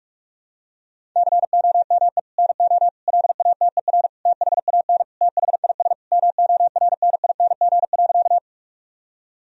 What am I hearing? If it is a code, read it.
Morse code, heard as a keyed tone, about 46 wpm, 700 Hz. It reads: COME NO LATER THAN THIS MORNING.
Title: Sentences from Top 1000 Words - Review Part 4 - 45wpm